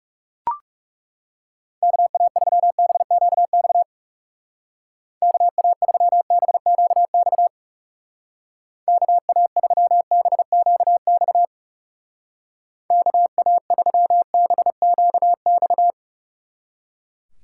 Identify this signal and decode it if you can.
Morse code, heard as a keyed tone, about 36 wpm, 700 Hz. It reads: KA3BQX KA3BQX KA3BQX KA3BQX
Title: Speed Racing - USA 2x3 Call Signs 30wpm